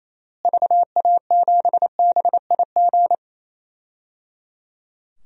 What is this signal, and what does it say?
Morse code, heard as a keyed tone, about 28 wpm, 700 Hz. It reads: VA7BIG
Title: DXCC-100 Call Signs - 6 Characters - Repeated 3x - 28wpm